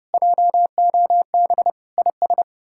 Morse code, heard as a keyed tone, about 37 wpm, 700 Hz. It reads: JOB IS
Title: Speed-Racing Common QSO Phrases - 30wpm